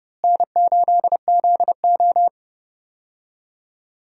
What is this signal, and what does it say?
Morse code, heard as a keyed tone, about 29 wpm, 700 Hz. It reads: N8ZO
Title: USA Call Signs - 4 Characters - Repeated 2x - 30wpm